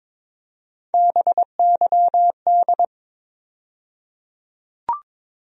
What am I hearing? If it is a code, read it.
Morse code, heard as a keyed tone, about 22 wpm, 700 Hz. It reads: BYD
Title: USA Call Sign Postfixes - 3 Letters 22wpm